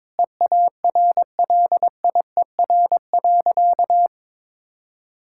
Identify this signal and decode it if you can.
Morse code, heard as a keyed tone, about 22 wpm, 700 Hz. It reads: EARLIER.
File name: practice / 22wpm / Sentences with 2-Word Phrases - 22wpm